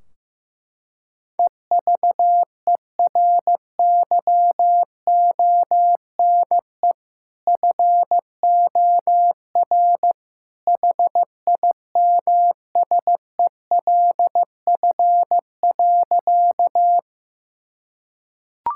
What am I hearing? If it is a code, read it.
Morse code, heard as a keyed tone, about 16 wpm, 700 Hz. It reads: EVERYONE FOR HIMSELF.
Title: Sentences from Top 1600 Words 15wpm